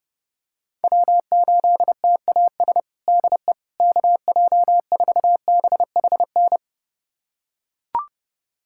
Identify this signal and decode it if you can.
Morse code, heard as a keyed tone, about 30 wpm, 700 Hz. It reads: W8TAS DE KJ4BHN
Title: USA Callsign DE Callsign 30wpm